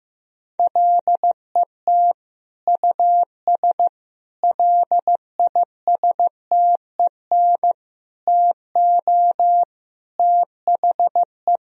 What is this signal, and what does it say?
Morse code, heard as a keyed tone, about 15 wpm, 700 Hz. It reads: LET US LISTEN TO THE
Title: Sentences from Top 1000 Words - Review Part 3 - 15wpm